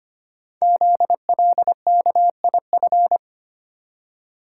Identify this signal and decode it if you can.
Morse code, heard as a keyed tone, about 25 wpm, 700 Hz. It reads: ZLKIF
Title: Code Groups - 5 Letters - 25wpm